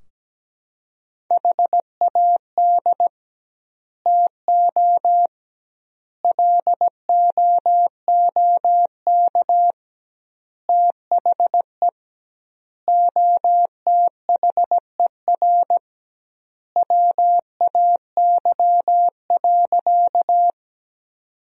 Morse code, heard as a keyed tone, about 17 wpm, 700 Hz. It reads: HAD TO LOOK THE OTHER WAY.
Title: 2x Word-Spacing - Sentences from Top 100 Words - 17wpm